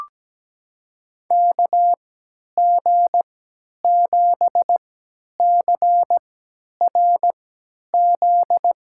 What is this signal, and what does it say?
Morse code, heard as a keyed tone, about 17 wpm, spaced as for 10 wpm, 700 Hz. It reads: KG7CRZ
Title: USA Call Signs - 3x Character Spacing - 17wpm